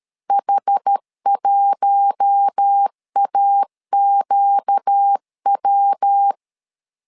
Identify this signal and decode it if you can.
Morse code, heard as a keyed tone, about 13 wpm, 800 Hz. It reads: H1AQW
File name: LEZIONI / 4 Q1 12wpm 2028